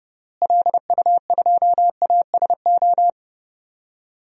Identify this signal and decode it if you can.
Morse code, heard as a keyed tone, about 30 wpm, 700 Hz. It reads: LU2ASO
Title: DXCC-100 Call Signs - 6 Characters - Repeated 3x - 30wpm